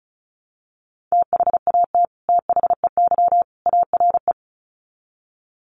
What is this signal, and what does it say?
Morse code, heard as a keyed tone, about 34 wpm, 700 Hz. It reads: THAT THEY ARE
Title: Sets of 3 Words 35wpm